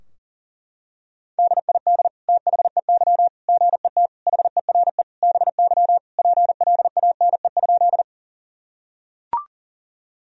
Morse code, heard as a keyed tone, about 37 wpm, 700 Hz. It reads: DID THEY GET HERE BY PLANE?
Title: Sentences from Top 1000 Words - Review Part 3 - 40wpm